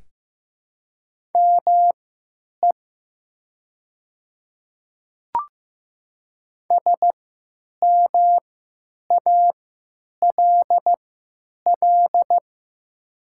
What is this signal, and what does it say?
Morse code, heard as a keyed tone, about 15 wpm, spaced as for 9 wpm, 700 Hz. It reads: ME SMALL
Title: 3x Character Spacing - Top 300 Words - 15wpm